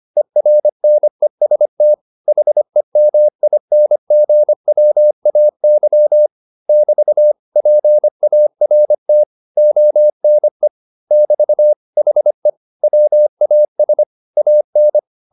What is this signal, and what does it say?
Morse code, heard as a keyed tone, about 25 wpm, 600 Hz. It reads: ERNEST HEMINGWAY <BT> PART ONE <BT> HE WAS AN